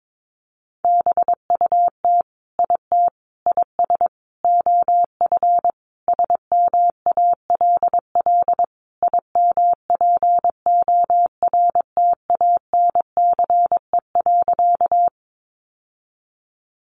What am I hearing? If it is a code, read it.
Morse code, heard as a keyed tone, about 22 wpm, 700 Hz. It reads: BUT IT IS OF SMALL IMPORTANCE.
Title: Sentences from Top 1200 Words 22wpm